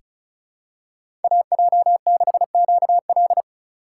Morse code, heard as a keyed tone, about 34 wpm, 700 Hz. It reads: AJ6QL
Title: K1USN SST QSO Practice - 35wpm